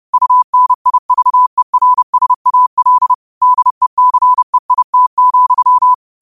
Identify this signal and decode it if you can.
Morse code, heard as a keyed tone, about 30 wpm, 1000 Hz. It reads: UNIVERSAL DECEIT,
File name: qotdfiles / 30wpm